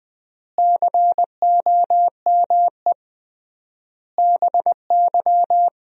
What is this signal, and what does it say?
Morse code, heard as a keyed tone, about 20 wpm, 700 Hz. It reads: COME BY
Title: 3x Word-Spacing - Sentences from Top 100 Words - 20wpm